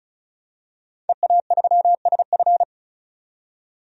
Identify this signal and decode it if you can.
Morse code, heard as a keyed tone, about 36 wpm, 700 Hz. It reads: EA3SF
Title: DXCC-100 Call Signs - 5 Characters - 35wpm